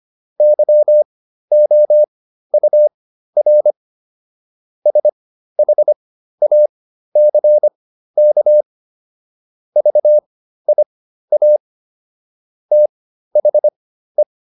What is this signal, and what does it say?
Morse code, heard as a keyed tone, about 25 wpm, spaced as for 13 wpm, 600 Hz. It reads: YOUR SHACK VIA THE